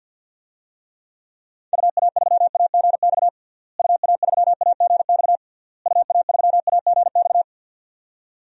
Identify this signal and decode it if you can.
Morse code, heard as a keyed tone, about 52 wpm, 700 Hz. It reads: UA3AGX UA3AGX UA3AGX
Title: DXCC-100 Call Signs - 6 Characters - Repeated 3x - 50wpm